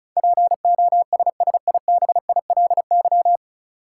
Morse code, heard as a keyed tone, about 36 wpm, 700 Hz. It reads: POSSIBILY
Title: Sentences from Top 1300 Words - New Word Drill Alt - 35wpm